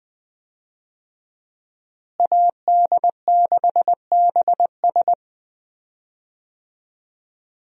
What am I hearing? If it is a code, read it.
Morse code, heard as a keyed tone, about 20 wpm, 700 Hz. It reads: AD6BS
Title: USA Call Signs - 5 Characters - Repeated 3x - 20wpm